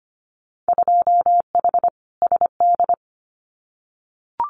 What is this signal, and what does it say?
Morse code, heard as a keyed tone, about 25 wpm, 700 Hz. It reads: 2H SD
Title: Winter Field Day Exchanges 25wpm